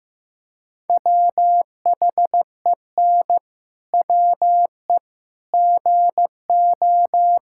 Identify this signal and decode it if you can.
Morse code, heard as a keyed tone, about 16 wpm, 700 Hz. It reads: WHEN WE GO
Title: Pangram Challenge v2 - 15wpm